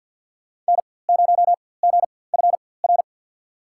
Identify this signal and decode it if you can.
Morse code, heard as a keyed tone, about 50 wpm, spaced as for 23 wpm, 700 Hz. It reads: N0GFR
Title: USA Call Signs - 5 Characters - 4x Character Spacing - 50wpm